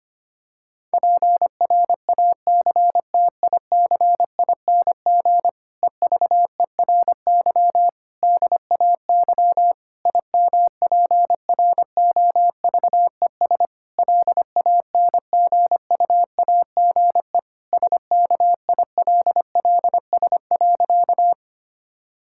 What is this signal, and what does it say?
Morse code, heard as a keyed tone, about 25 wpm, 700 Hz. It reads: PRACTICING EVERY DAY IMPROVES LANGUAGE SKILLS.
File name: practice / 25wpm / Difficult Sentences with Hints - 25wpm